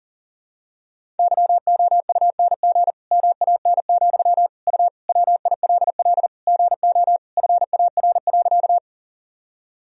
Morse code, heard as a keyed tone, about 40 wpm, 700 Hz. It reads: YOUNG MAN, U WILL GO FAR.
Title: Sentences from Top 400 Words and Abbreviations 40wpm